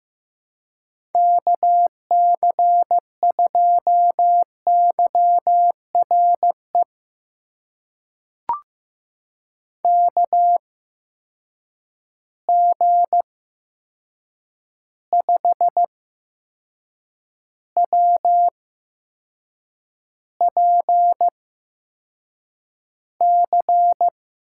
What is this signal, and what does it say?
Morse code, heard as a keyed tone, about 15 wpm, spaced as for 6 wpm, 700 Hz. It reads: KC2YRE KG5WPC
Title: USA Call Signs - 6 Characters - 8x Character Spacing - 15wpm